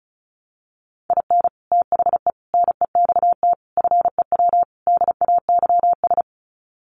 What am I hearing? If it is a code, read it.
Morse code, heard as a keyed tone, about 34 wpm, 700 Hz. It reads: IN THE NEXT FEW DAYS
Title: Sets of 5 Words Encore2 35wpm